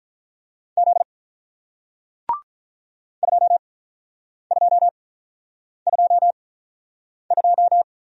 Morse code, heard as a keyed tone, about 44 wpm, spaced as for 9 wpm, 700 Hz. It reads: Z 2222